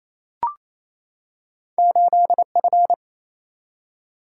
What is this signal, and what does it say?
Morse code, heard as a keyed tone, about 28 wpm, 700 Hz. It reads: 8F
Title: Single-Letter Single-Number 28wpm